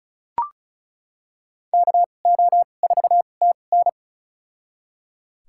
Morse code, heard as a keyed tone, about 36 wpm, spaced as for 25 wpm, 700 Hz. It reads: KO4TN
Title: USA Call Signs - 5 Characters - 2x Character Spacing - 35wpm